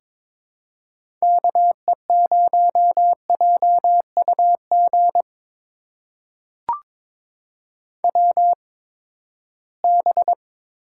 Morse code, heard as a keyed tone, about 22 wpm, spaced as for 15 wpm, 700 Hz. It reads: KE0JUG WB
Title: USA Call Signs - 6 Characters - 8x Character Spacing - 22wpm